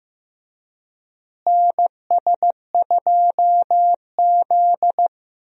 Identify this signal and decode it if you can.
Morse code, heard as a keyed tone, about 15 wpm, 700 Hz. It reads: NS2Z
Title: USA Call Signs - 4 Characters - Repeated 2x - 15wpm